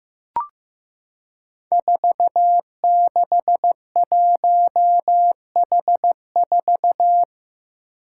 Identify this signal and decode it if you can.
Morse code, heard as a keyed tone, about 15 wpm, 700 Hz. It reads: 461H4
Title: Five Challenging Letter-Number Combinations 15wpm